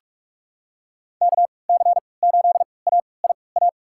Morse code, heard as a keyed tone, about 46 wpm, spaced as for 26 wpm, 700 Hz. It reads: KC8AIA
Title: USA Call Signs - 3x Character Spacing - 45wpm